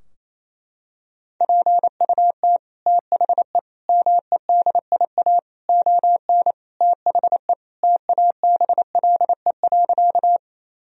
Morse code, heard as a keyed tone, about 27 wpm, 700 Hz. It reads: PUT THE MEDIA ON THE TABLE.
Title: Sentences from Top 1000 Words - Review Part 4 - 28wpm